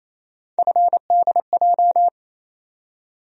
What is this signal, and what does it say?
Morse code, heard as a keyed tone, about 28 wpm, 700 Hz. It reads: FDJ
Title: USA Call Sign Postfixes - 3 Letters 28wpm